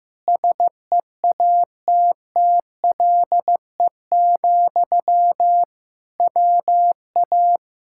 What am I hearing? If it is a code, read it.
Morse code, heard as a keyed tone, about 15 wpm, 700 Hz. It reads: SEATTLE, WA
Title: Speed-Racing Common QSO Phrases Encore - 15wpm